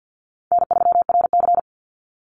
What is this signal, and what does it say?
Morse code, heard as a keyed tone, about 50 wpm, 700 Hz. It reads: N3RC
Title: Easy Call Signs - 50wpm